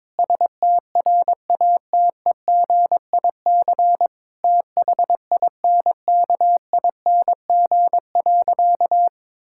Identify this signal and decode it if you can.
Morse code, heard as a keyed tone, about 22 wpm, 700 Hz. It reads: STRATEGIC THINKING.
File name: Difficult Sentences with Hints - 22wpm